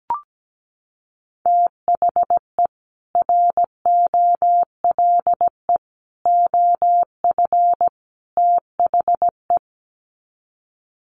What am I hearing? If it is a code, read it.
Morse code, heard as a keyed tone, about 17 wpm, 700 Hz. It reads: THE ROLE OF THE
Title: Sets of 4 Words 17wpm